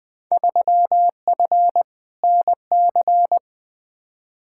Morse code, heard as a keyed tone, about 20 wpm, 700 Hz. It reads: 3F NC
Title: Field Day Exchanges - 20wpm